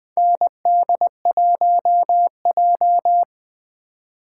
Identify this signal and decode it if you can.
Morse code, heard as a keyed tone, about 20 wpm, 700 Hz. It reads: ND1J
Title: 2023 SOTA Chaser Call Signs - 20wpm